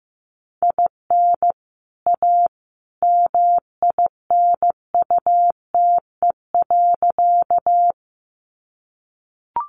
Morse code, heard as a keyed tone, about 15 wpm, 700 Hz. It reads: IN A MINUTE.
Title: Sentences from Top 500 Words 15wpm